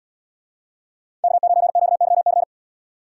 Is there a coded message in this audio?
Yes: Morse code from 1.2 to 2.4 s, heard as a keyed tone, about 77 wpm, 700 Hz. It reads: K9QYC